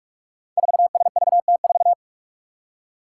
Morse code, heard as a keyed tone, about 44 wpm, 700 Hz. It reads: 4SVT4
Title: Five Challenging Letter-Number Combinations 45wpm